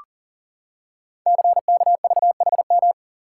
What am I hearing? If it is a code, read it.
Morse code, heard as a keyed tone, about 39 wpm, 700 Hz. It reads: CKVHM
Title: Code Groups - 5 Letters - 40wpm